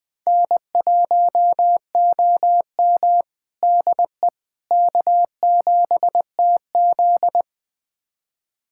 Morse code, heard as a keyed tone, about 20 wpm, 700 Hz. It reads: N1OM DE K7TZ
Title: USA Callsign DE Callsign - 4 Characters - 20wpm